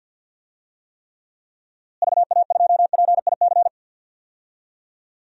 Morse code, heard as a keyed tone, about 51 wpm, 700 Hz. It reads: VA2PIC
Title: DXCC-100 Call Signs - 6 Characters - Repeated 3x - 50wpm